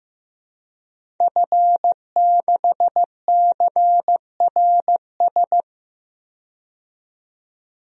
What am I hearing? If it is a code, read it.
Morse code, heard as a keyed tone, about 15 wpm, 700 Hz. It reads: F6CRS
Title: DXCC-100 Call Signs - 5 Characters - 15wpm